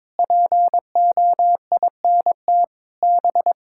Morse code, heard as a keyed tone, about 22 wpm, 700 Hz. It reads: POINT B
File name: practice / 22wpm / Sets of 8 Words - 22wpm